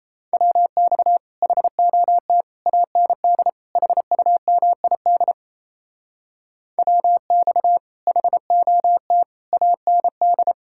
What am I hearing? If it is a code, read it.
Morse code, heard as a keyed tone, about 34 wpm, 700 Hz. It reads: WX HOT AND HUMID WX HOT AND
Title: Speed-Racing Common QSO Phrases Encore - 28wpm